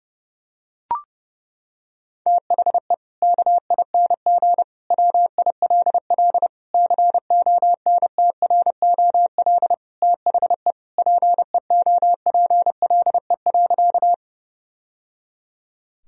Morse code, heard as a keyed tone, about 30 wpm, 700 Hz. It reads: THE KING WILL CONTROL THE PEOPLE.
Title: Sentences from Top 1000 Words - Review Part 3 - 30wpm